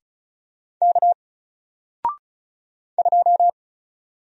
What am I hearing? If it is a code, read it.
Morse code, heard as a keyed tone, about 34 wpm, 700 Hz. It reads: K 2
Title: Single Letter-Number 35wpm